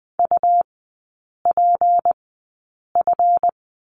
Morse code, heard as a keyed tone, about 20 wpm, spaced as for 8 wpm, 700 Hz. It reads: UPF
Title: 152 Lesson 25e - Letter p - Three Characters 20wpm